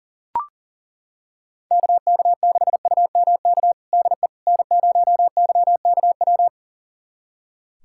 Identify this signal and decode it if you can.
Morse code, heard as a keyed tone, about 40 wpm, 700 Hz. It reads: KK6UMK DE N0YKW
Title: USA Callsign DE Callsign 40wpm